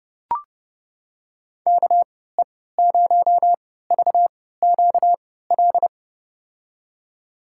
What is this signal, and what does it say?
Morse code, heard as a keyed tone, about 30 wpm, spaced as for 17 wpm, 700 Hz. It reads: KE0VQL
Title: USA Call Signs - 6 Characters - 3x Character Spacing - 30wpm